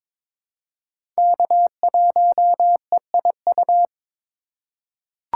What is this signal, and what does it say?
Morse code, heard as a keyed tone, about 22 wpm, 700 Hz. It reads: K1EIU
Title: USA Call Signs - 5 Characters - 8x Character Spacing - 22wpm